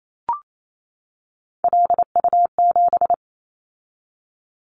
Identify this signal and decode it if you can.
Morse code, heard as a keyed tone, about 28 wpm, 700 Hz. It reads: LU7
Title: Callsign Prefixes for Top 100 DXCC - 28wpm